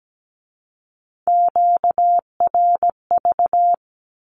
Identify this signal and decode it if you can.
Morse code, heard as a keyed tone, about 17 wpm, 700 Hz. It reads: QRV
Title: Q-Codes with Spoken Meaning - 17wpm